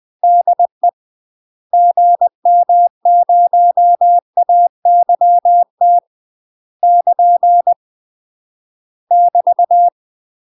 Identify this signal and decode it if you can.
Morse code, heard as a keyed tone, about 20 wpm, 700 Hz. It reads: DE GM0AYT <KN> <BT>
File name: UK MASTER QSOs TRAINING / 20WPM / UK MASTER QSOs 0003